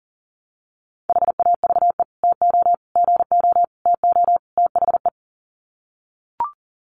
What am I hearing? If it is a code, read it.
Morse code, heard as a keyed tone, about 40 wpm, 700 Hz. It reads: HAVE TO GO TO THE
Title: Sets of 5 Words Encore 40wpm